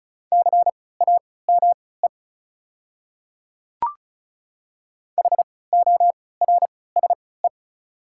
Morse code, heard as a keyed tone, about 34 wpm, spaced as for 20 wpm, 700 Hz. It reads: CAME HORSE
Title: 3x Character Spacing - Top 300 Words - 35wpm